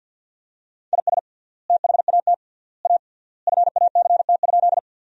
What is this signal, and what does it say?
Morse code, heard as a keyed tone, about 50 wpm, 700 Hz. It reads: IS THAT A FACT?
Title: 3x Word-Spacing - Sentences from Top 500 Words - 50wpm